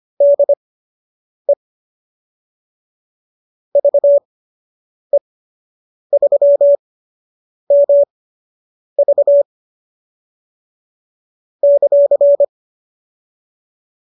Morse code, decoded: DE VE3MV ;

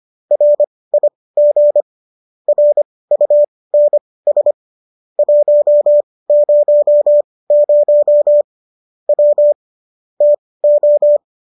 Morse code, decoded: RIG RUNS 100 W TO

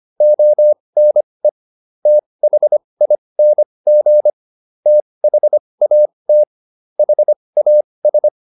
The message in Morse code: ONE THING THAT HAS

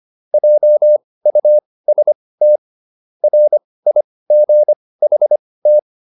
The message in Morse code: JUST RIGHT

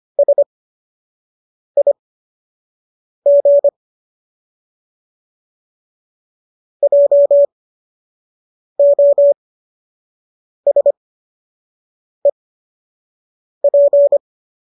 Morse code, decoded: SIG JOSEP

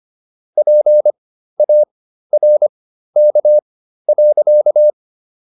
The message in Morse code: PARK.

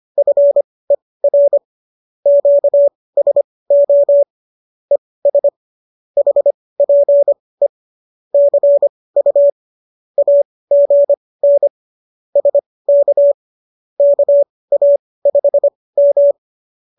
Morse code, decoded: FER QSO ES HPE CU AGN SK KA5M

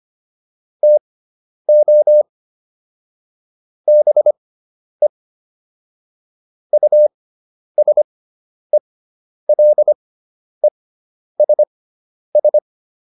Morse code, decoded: TO BE USELESS

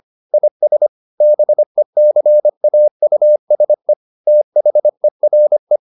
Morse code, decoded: IS BECAUSE THERE

